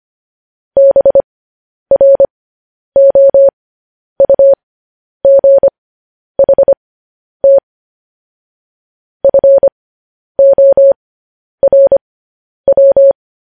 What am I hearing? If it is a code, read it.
Morse code, heard as a keyed tone, about 25 wpm, spaced as for 10 wpm, 550 Hz. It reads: BROUGHT FORW
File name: podcast / News headlines-1x-25x10